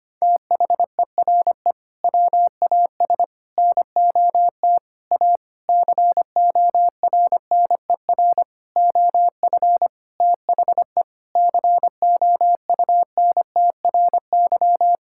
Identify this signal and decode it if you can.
Morse code, heard as a keyed tone, about 25 wpm, 700 Hz. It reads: THERE WAS NOT A CORNER OF THE COUNTRY
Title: Sentences from Top 1600 Words 25wpm